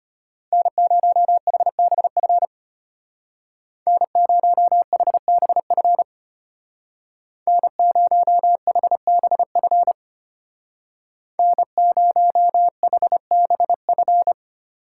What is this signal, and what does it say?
Morse code, heard as a keyed tone, about 32 wpm, 700 Hz. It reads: N0HBF N0HBF N0HBF N0HBF